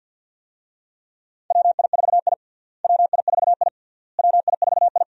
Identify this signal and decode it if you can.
Morse code, heard as a keyed tone, about 51 wpm, 700 Hz. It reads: WI4I WI4I WI4I
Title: USA Call Signs - 4 Characters - Repeated 3x - 50wpm